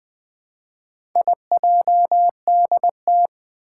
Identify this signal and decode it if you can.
Morse code, heard as a keyed tone, about 20 wpm, 700 Hz. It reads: IJDT
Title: Four Challenging Letter-Number Combinations 20wpm